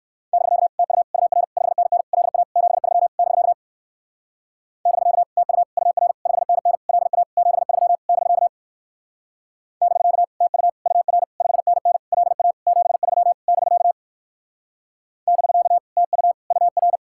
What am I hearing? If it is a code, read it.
Morse code, heard as a keyed tone, about 60 wpm, 700 Hz. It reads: <BK> TU UR 5NN LA 73 <BK> <BK> TU UR 5NN LA 73 <BK> <BK> TU UR 5NN LA 73 <BK> <BK> TU UR